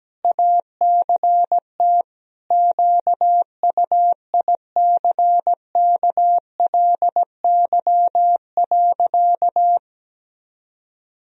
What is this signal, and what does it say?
Morse code, heard as a keyed tone, about 17 wpm, 700 Hz. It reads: ACT QUICKLY.
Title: Sentences from Top 1000 Words - Review Part 4 - 17wpm